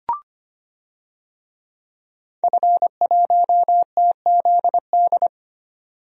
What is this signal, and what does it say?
Morse code, heard as a keyed tone, about 25 wpm, 700 Hz. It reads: F1TZD
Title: DXCC-100 Call Signs - 5 Characters - 25wpm